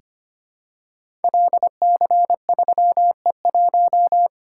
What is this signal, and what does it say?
Morse code, heard as a keyed tone, about 25 wpm, 700 Hz. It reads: LC3E1
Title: Code Groups - 5 Letter:Numbers - 25wpm